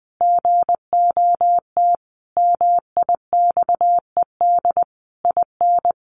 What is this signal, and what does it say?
Morse code, heard as a keyed tone, about 19 wpm, 700 Hz. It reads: GOT MIXED IN
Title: Sentences from Top 1500 Words 20wpm